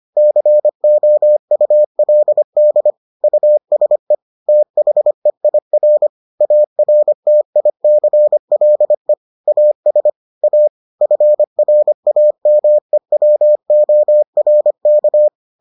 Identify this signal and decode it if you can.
Morse code, heard as a keyed tone, about 25 wpm, 600 Hz. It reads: COULD USE THEIR ARTICLE AS A FRAMEWORK